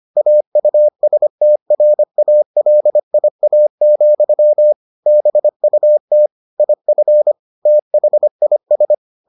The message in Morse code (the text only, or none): AUSTRALIA, BUT IF THIS